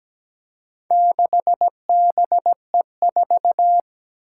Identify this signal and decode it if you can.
Morse code, heard as a keyed tone, about 17 wpm, 700 Hz. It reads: 6BE4
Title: Four Challenging Letter-Number Combinations 17wpm